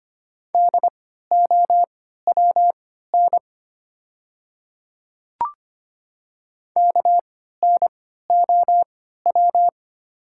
Morse code, heard as a keyed tone, about 25 wpm, spaced as for 14 wpm, 700 Hz. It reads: DOWN KNOW
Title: 3x Character Spacing - Top 100 Words - 25wpm